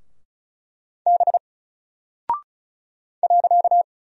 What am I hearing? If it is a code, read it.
Morse code, heard as a keyed tone, about 36 wpm, 700 Hz. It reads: B .
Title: Single Character (Letter:Number:Punctuation:Prosign) 35wpm